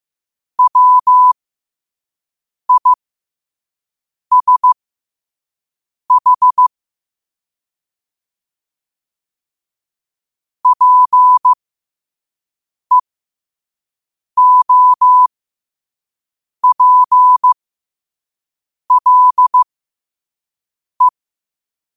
Morse code, heard as a keyed tone, about 15 wpm, spaced as for 5 wpm, 1000 Hz. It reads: WISH PEOPLE
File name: qotdfiles / 5wpm